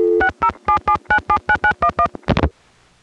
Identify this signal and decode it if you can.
DTMF, heard as telephone touch tones, tones of about 80 ms. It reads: 6077979911